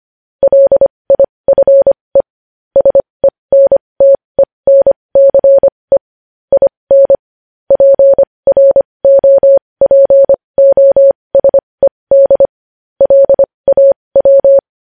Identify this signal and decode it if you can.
Morse code, heard as a keyed tone, about 25 wpm, spaced as for 20 wpm, 550 Hz. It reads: LIFE SENTENCE IN PROPOSED LAW
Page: news headlines in morse code at 20 wpm